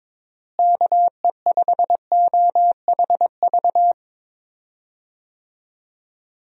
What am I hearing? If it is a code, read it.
Morse code, heard as a keyed tone, about 22 wpm, 700 Hz. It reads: KE5OHV
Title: USA Call Signs - 6 Characters - Repeated 3x - 22wpm